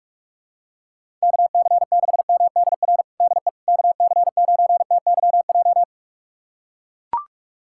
Morse code, heard as a keyed tone, about 42 wpm, 700 Hz. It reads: KC6MDR DE KC9TYJ
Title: USA Callsign DE Callsign - 6 Characters - 45wpm